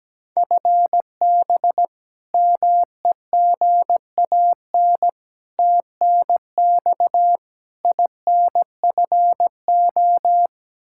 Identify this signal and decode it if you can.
Morse code, heard as a keyed tone, about 17 wpm, 700 Hz. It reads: FB MEGAN TNX INFO
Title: Practice QSO - Exchange 3 - More Info - 17wpm